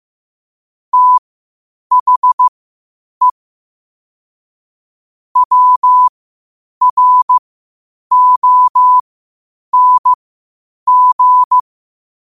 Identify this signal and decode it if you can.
Morse code, heard as a keyed tone, about 15 wpm, spaced as for 8 wpm, 1000 Hz. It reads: THE WRONG
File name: qotdfiles / 8wpm